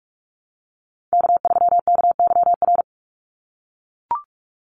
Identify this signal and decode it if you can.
Morse code, heard as a keyed tone, about 44 wpm, 700 Hz. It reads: K3KYR
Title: NA SOTA Chaser Callsigns 45wpm